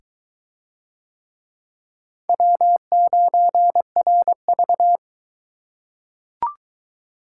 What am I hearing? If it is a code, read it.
Morse code, heard as a keyed tone, about 20 wpm, 700 Hz. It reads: W9RV